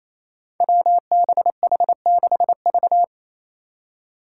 Morse code, heard as a keyed tone, about 27 wpm, 700 Hz. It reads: WBH6V
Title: Five Challenging Letter-Number Combinations 28wpm